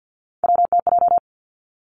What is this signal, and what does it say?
Morse code, heard as a keyed tone, about 50 wpm, 700 Hz. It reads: WTJ
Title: Three Challenging Letter-Number Combinations 50wpm